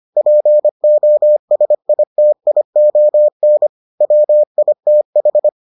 Morse code, heard as a keyed tone, about 25 wpm, 600 Hz. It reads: POSITION WITH